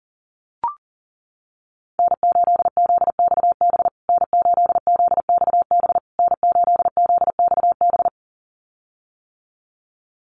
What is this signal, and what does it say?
Morse code, heard as a keyed tone, about 40 wpm, 700 Hz. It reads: N8ZXB N8ZXB N8ZXB
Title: USA Call Signs Repeated 3x 40wpm